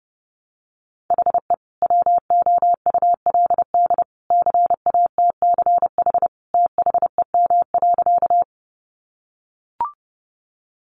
Morse code, heard as a keyed tone, about 30 wpm, 700 Hz. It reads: HE WOULD CATCH THEM.
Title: Sentences from Top 1500 Words 30wpm